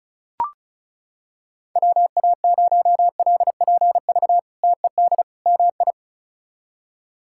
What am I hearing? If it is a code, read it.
Morse code, heard as a keyed tone, about 34 wpm, 700 Hz. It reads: WA0LPV TED MI
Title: K1USN SST QSO Practice - 35wpm